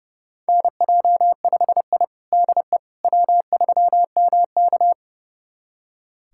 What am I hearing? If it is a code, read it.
Morse code, heard as a keyed tone, about 30 wpm, 700 Hz. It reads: NJ5I DE W3MK
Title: USA Callsign DE Callsign - 4 Characters - 30wpm